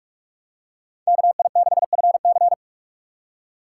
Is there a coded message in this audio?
Yes: Morse code from 1.1 to 2.5 s, heard as a keyed tone, about 46 wpm, 700 Hz. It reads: KI6FC